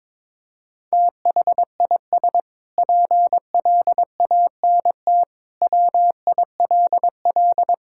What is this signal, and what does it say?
Morse code, heard as a keyed tone, about 22 wpm, 700 Hz. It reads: THIS PLANT WILL
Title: Sentences from Top 1000 Words - Review Part 6 - 22wpm